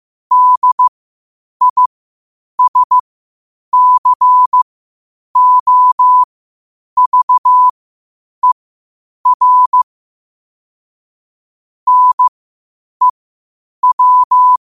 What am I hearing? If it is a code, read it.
Morse code, heard as a keyed tone, about 15 wpm, spaced as for 8 wpm, 1000 Hz. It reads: DISCOVER NEW
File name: qotdfiles / 8wpm